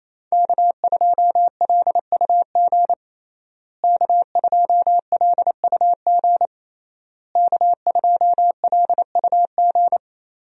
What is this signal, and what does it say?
Morse code, heard as a keyed tone, about 28 wpm, 700 Hz. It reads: K2LUG K2LUG K2LUG
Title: USA Call Signs - 5 Characters - Repeated 3x - 28wpm